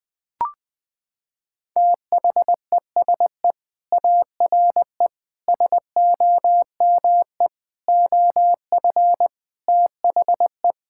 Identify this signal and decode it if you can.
Morse code, heard as a keyed tone, about 19 wpm, 700 Hz. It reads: THESE ARE SOME OF THE
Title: Sentences with 4-Word Phrases - 20wpm